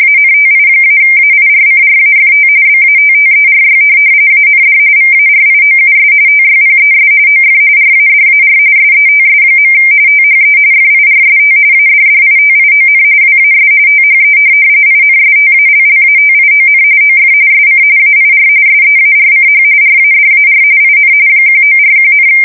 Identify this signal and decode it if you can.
RTTY, about 45.45 baud. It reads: ANTI-AIRCRAFT BATTERY AND VARIOUS TYPES OF RADAR SETS. THE FIRST ADDITION WAS THE INSTALLATION OF SC AIR SEARCH RADAR IN 1941,